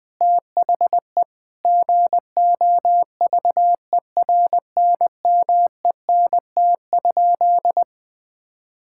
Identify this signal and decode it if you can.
Morse code, heard as a keyed tone, about 20 wpm, 700 Hz. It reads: THE GOVERNMENT?
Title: Sentences from Top 1000 Words - Review Part 5 - 20wpm